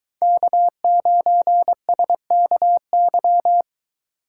Morse code, heard as a keyed tone, about 23 wpm, 700 Hz. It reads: K9SKY